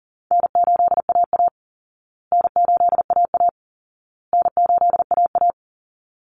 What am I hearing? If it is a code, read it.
Morse code, heard as a keyed tone, about 40 wpm, 700 Hz. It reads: N8AA N8AA N8AA
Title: Call Signs Repeated 2x - 40wpm